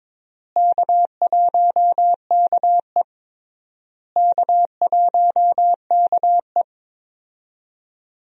Morse code, heard as a keyed tone, about 22 wpm, 700 Hz. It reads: K1KE K1KE
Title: USA Call Signs - 4 Characters - Repeated 3x - 22wpm